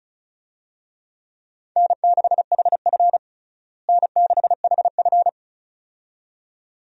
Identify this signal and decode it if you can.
Morse code, heard as a keyed tone, about 36 wpm, 700 Hz. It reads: N6HF N6HF
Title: USA Call Signs - 4 Characters - Repeated 2x - 35wpm